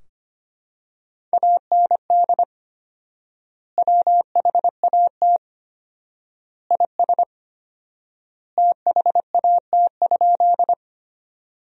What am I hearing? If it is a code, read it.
Morse code, heard as a keyed tone, about 25 wpm, 700 Hz. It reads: AND WHAT IS THAT?
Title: 4x Word-Spacing - Sentences from Top 100 Words - 25wpm